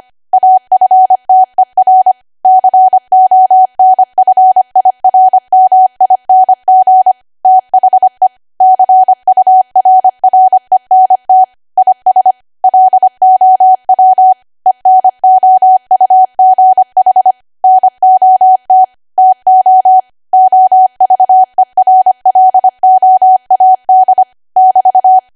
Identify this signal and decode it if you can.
Morse code, heard as a keyed tone, about 25 wpm, 750 Hz. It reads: AFTER CONFIRMING THE CURRENT IS LOW ENOUGH NOT TO OVERLOAD <BT>